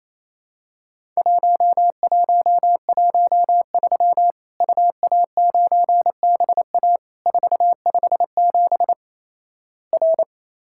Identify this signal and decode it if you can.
Morse code, heard as a keyed tone, about 28 wpm, 700 Hz. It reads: 1113 UA9BA 457 R
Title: Worked All Europe DX Contest - QTC Exchange - 28wpm